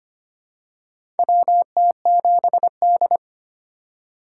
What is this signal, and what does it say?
Morse code, heard as a keyed tone, about 25 wpm, 700 Hz. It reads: WT7D
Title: USA Call Signs - 4 Characters - 8x Character Spacing - 25wpm